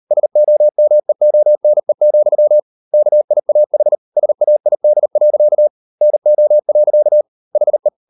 Morse code, heard as a keyed tone, about 38 wpm, 600 Hz. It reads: SOMEONE, KIAH SAID. NO. HE